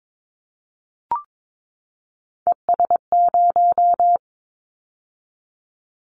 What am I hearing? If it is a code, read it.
Morse code, heard as a keyed tone, about 22 wpm, 700 Hz. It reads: ES0
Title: Callsign Prefixes for Top 100 DXCC - 22wpm